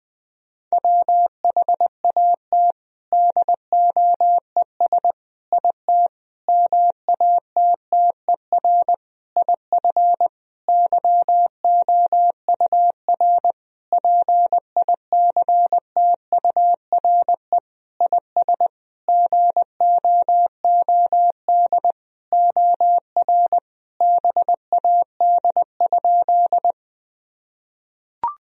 Morse code, heard as a keyed tone, about 20 wpm, 700 Hz. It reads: WHAT DOES IT MATTER IF YOUR PICTURE IS GOOD OR BAD?
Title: Sentences from Top 1000 Words - Review Part 1 - 20wpm